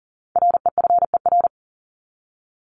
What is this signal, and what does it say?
Morse code, heard as a keyed tone, about 40 wpm, 700 Hz. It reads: REFER
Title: Top 751-1000 Words 40wpm